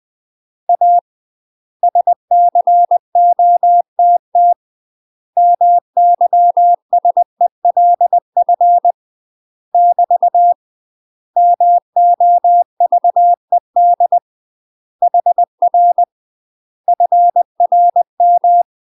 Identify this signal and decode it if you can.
Morse code, heard as a keyed tone, about 20 wpm, 700 Hz. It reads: A SCOTT MYSELF <BT> MOVED HR FRM
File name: UK MASTER QSOs TRAINING / 20WPM / UK MASTER QSOs 0063